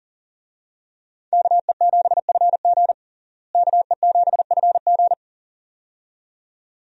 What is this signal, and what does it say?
Morse code, heard as a keyed tone, about 40 wpm, 700 Hz. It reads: KE7FG KE7FG
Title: USA Call Signs - 5 Characters - Repeated 2x - 40wpm